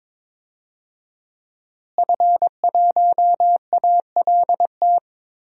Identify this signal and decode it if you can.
Morse code, heard as a keyed tone, about 22 wpm, 700 Hz. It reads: F1ALT
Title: DXCC-100 Call Signs - 5 Characters - Repeated 3x - 22wpm